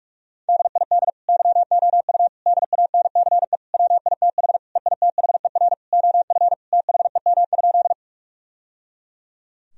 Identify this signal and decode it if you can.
Morse code, heard as a keyed tone, about 41 wpm, 700 Hz. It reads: DID YOU DANCE WITH EITHER OF THEM?
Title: Sentences from Top 1000 Words - Review Part 4 - 45wpm